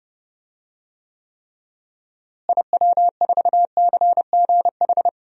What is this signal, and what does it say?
Morse code, heard as a keyed tone, about 29 wpm, 700 Hz. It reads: IW4CGH
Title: DXCC-100 Call Signs - 6 Characters - Repeated 3x - 30wpm